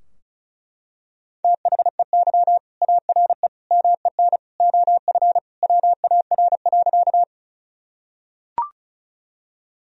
Morse code, heard as a keyed tone, about 36 wpm, 700 Hz. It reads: THEY ARE MEN OF WAR.
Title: Sentences from Top 1000 Words - Review Part 2 - 35wpm